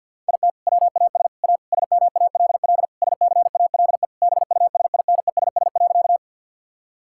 Morse code, heard as a keyed tone, about 52 wpm, 700 Hz. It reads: IT WAS A SMALL SCALE BUSINESS.
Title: Sentences from Top 1000 Words - Review Part 5 - 50wpm